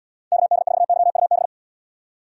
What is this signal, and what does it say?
Morse code, heard as a keyed tone, about 76 wpm, 700 Hz. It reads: KD4YUB